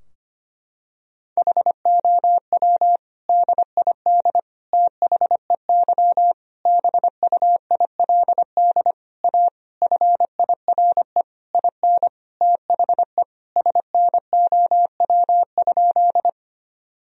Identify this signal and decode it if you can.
Morse code, heard as a keyed tone, about 25 wpm, 700 Hz. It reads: HOW DID THEY BUILD A FIRE IN THE SNOW?
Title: Sentences from Top 1000 Words - Review Part 3 - 25wpm